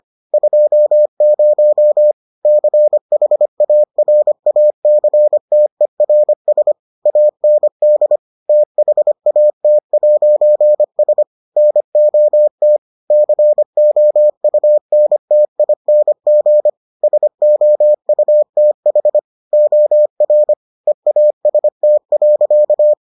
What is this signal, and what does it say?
Morse code, heard as a keyed tone, about 25 wpm, 600 Hz. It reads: 20 CHARACTERS AND THAT'S NOT COUNTING SOUTH OR EAST.